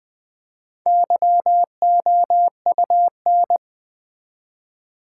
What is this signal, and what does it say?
Morse code, heard as a keyed tone, about 20 wpm, 700 Hz. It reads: YOUN
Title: Build A Word - 1 Letter At A Time - 5-Letter Words - 20wpm